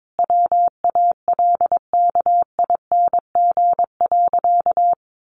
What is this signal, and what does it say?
Morse code, heard as a keyed tone, about 22 wpm, 700 Hz. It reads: WALKING.
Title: Easy Phrases 22wpm